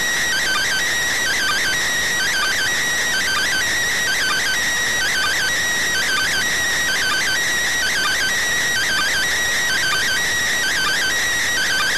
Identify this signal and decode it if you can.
Two-tone teleprinter signal, about 400 Hz shift.